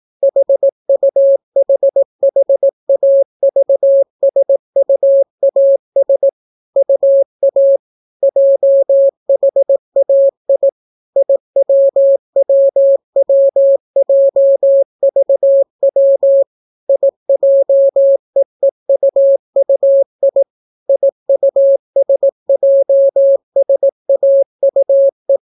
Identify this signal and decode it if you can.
Morse code, heard as a keyed tone, about 18 wpm, 550 Hz. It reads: HUHHAVSUAS UA JHAI IWWWJVW IJEEUUI IUSJSAUE